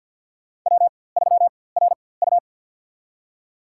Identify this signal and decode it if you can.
Morse code, heard as a keyed tone, about 53 wpm, spaced as for 24 wpm, 700 Hz. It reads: W3RU
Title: USA Call Signs - 4x Character Spacing - 50wpm